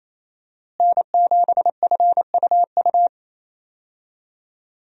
Morse code, heard as a keyed tone, about 28 wpm, 700 Hz. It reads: N7FUU
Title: USA Call Signs - 5 Characters - Repeated 3x - 28wpm